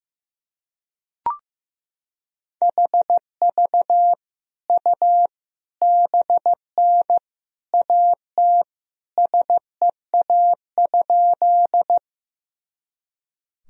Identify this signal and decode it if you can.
Morse code, heard as a keyed tone, about 15 wpm, 700 Hz. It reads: HV U BN AT SEA?
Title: Sentences from Top 300 Words and Abbreviations 15wpm